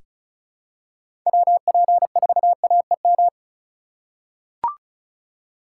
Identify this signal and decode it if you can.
Morse code, heard as a keyed tone, about 34 wpm, 700 Hz. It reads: WP4AEM
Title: USA Call Signs - 6 Characters - 4x Character Spacing - 35wpm